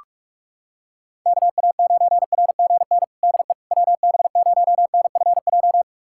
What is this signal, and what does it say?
Morse code, heard as a keyed tone, about 41 wpm, 700 Hz. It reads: KA9RGN DE WB0NFJ
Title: USA Callsign DE Callsign - 6 Characters - 45wpm